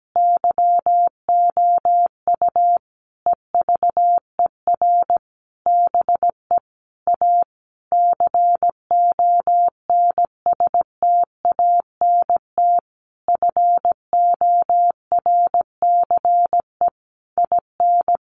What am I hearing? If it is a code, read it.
Morse code, heard as a keyed tone, about 17 wpm, 700 Hz. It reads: YOU EVER BE A CONSTANT FORCE IN
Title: Sentences from Top 1500 Words 17wpm